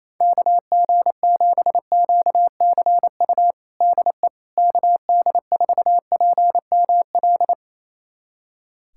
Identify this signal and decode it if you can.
Morse code, heard as a keyed tone, about 28 wpm, 700 Hz. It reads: KG7QCU DE KD4PML
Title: USA Callsign DE Callsign - 6 Characters - 28wpm